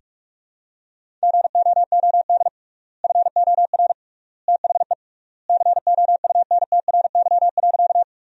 Morse code, heard as a keyed tone, about 44 wpm, 700 Hz. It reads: GOOD FOR THE COUNTRY.
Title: 3x Word-Spacing - Sentences from Top 300 Words - 45wpm